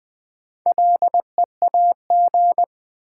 Morse code, heard as a keyed tone, about 20 wpm, 700 Hz. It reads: LEAG
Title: Build A Word - 2 Letters At A Time - 6-Letter Words - 20wpm